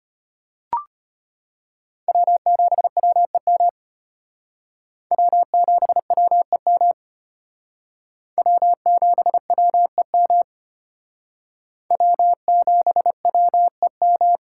Morse code, heard as a keyed tone, about 31 wpm, 700 Hz. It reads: W7WEM W7WEM W7WEM W7WEM
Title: Speed Racing - USA 1x3 Call Signs 25wpm